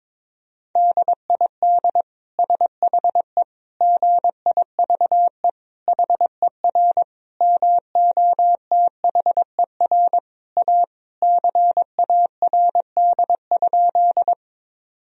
Morse code, heard as a keyed tone, about 22 wpm, 700 Hz. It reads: DID SHE GIVE HER MOTHER A CARD?
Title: Sentences from Top 1000 Words - Review Part 4 - 22wpm